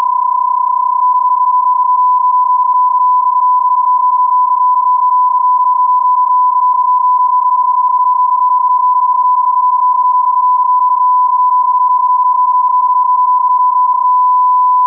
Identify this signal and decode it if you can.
PSK31, about 31.25 baud.